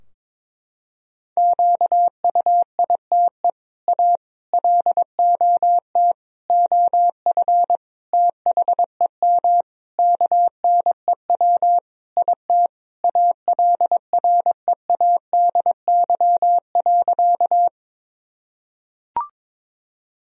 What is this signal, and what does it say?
Morse code, heard as a keyed tone, about 22 wpm, 700 Hz. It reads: QUITE A LOT OF THEM KNEW IT ALREADY.
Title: Sentences from Top 1000 Words - Review Part 3 - 22wpm